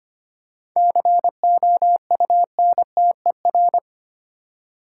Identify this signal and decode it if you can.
Morse code, heard as a keyed tone, about 25 wpm, 700 Hz. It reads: COUNTER
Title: Seven Letter Words - 25wpm